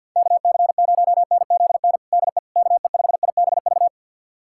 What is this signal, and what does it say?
Morse code, heard as a keyed tone, about 52 wpm, 700 Hz. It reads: KC0NZN DE KE5IBV
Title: USA Callsign DE Callsign 50wpm